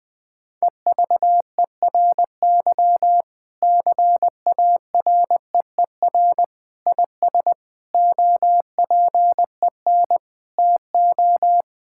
Morse code, heard as a keyed tone, about 21 wpm, 700 Hz. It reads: EVERY CAREER IS OPEN TO
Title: Sentences from Top 1000 Words - Review Part 6 - 20wpm